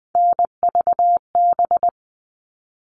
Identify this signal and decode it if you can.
Morse code, heard as a keyed tone, about 20 wpm, 700 Hz. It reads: NVB
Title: All Cut Numbers - 3-Digit 20wpm